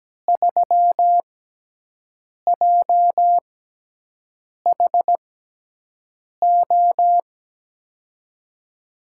Morse code, heard as a keyed tone, about 17 wpm, 700 Hz. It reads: 3 J H O
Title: USA Call Signs - 6 Characters - 6x Character Spacing - 17wpm